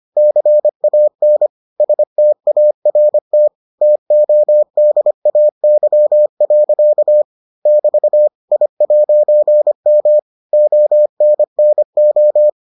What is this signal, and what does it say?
Morse code, heard as a keyed tone, about 25 wpm, 600 Hz. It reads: CAN START TODAY. <BT> I'M ONNO